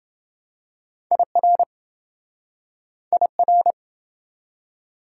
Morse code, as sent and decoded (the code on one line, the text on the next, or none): .. .-. / .. .-.
IR IR